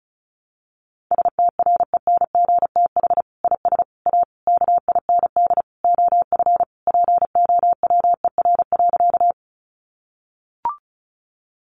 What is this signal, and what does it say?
Morse code, heard as a keyed tone, about 34 wpm, 700 Hz. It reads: STRENGTH IS A KIND OF POWER.
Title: Sentences from Top 1300 Words 35wpm